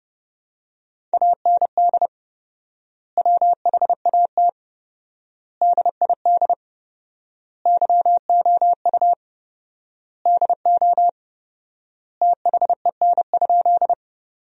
Morse code, heard as a keyed tone, about 30 wpm, 700 Hz. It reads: AND WHAT DID YOU DO THEN?
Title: 4x Word-Spacing - Sentences from Top 100 Words - 30wpm